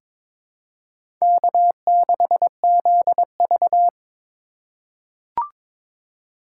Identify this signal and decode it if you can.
Morse code, heard as a keyed tone, about 22 wpm, 700 Hz. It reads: K6ZV
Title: USA Call Signs - 2x Character Spacing - 22wpm